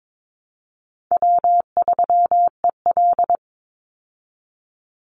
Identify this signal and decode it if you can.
Morse code, heard as a keyed tone, about 22 wpm, 700 Hz. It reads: W3EL
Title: Easy Call Signs Repeated 3x - 22wpm